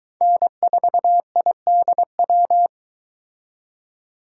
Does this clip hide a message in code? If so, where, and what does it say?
Yes: Morse code from 0.2 to 2.7 s, heard as a keyed tone, about 22 wpm, 700 Hz. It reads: N4IDW